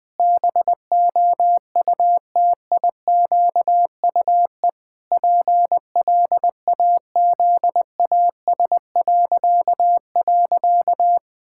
Morse code, heard as a keyed tone, about 20 wpm, 700 Hz. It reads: BOUTIQUE PLAZAS..
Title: Pangram Challenge v2 - 20wpm